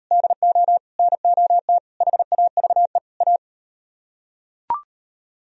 Morse code, heard as a keyed tone, about 37 wpm, 700 Hz. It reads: DO NOT HAVE A